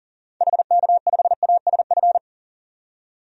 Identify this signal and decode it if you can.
Morse code, heard as a keyed tone, about 40 wpm, 700 Hz. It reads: HK5ASF
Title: DXCC-100 Call Signs - 6 Characters - 40wpm